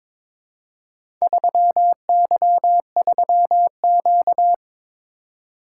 Morse code, heard as a keyed tone, about 22 wpm, 700 Hz. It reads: 3Y3Q
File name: practice / 22wpm / Four Letter-Number Combinations - 22wpm